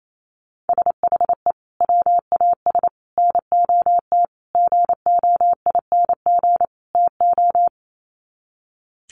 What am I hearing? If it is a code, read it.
Morse code, heard as a keyed tone, about 28 wpm, 700 Hz. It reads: SHE WAS NOT GOING TO
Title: Sets of 5 Words Encore2 28wpm